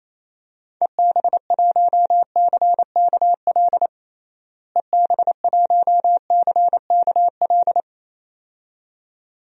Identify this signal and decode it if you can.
Morse code, heard as a keyed tone, about 28 wpm, 700 Hz. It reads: EB1CKL EB1CKL
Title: DXCC-100 Call Signs - 6 Characters - Repeated 2x - 28wpm